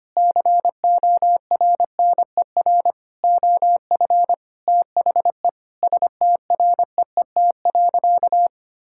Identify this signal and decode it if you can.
Morse code, heard as a keyed tone, about 25 wpm, 700 Hz. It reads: CORNER OF THE STREET.
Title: Sentences from Top 1600 Words 25wpm